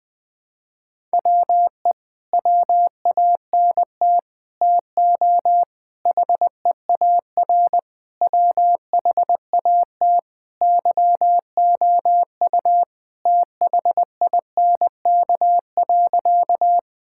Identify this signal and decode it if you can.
Morse code, heard as a keyed tone, about 20 wpm, 700 Hz. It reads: WE WANT TO HEAR WHAT YOU THINK.
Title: Sentences with 6-Word Phrases - 20wpm